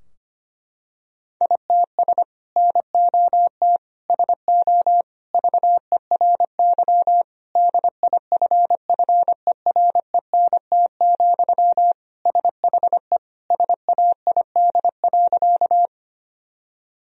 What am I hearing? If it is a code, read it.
Morse code, heard as a keyed tone, about 25 wpm, 700 Hz. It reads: ITS NOT SO VERY DIFFERENT, SHE SAID.
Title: Sentences from Top 1000 Words - Review Part 3 - 25wpm